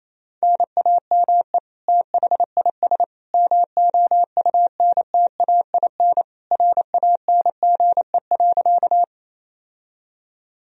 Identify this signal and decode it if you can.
Morse code, heard as a keyed tone, about 28 wpm, 700 Hz. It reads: NAME THIS MOUNTAIN RANGE.
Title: Sentences from Top 1000 Words - Review Part 4 - 28wpm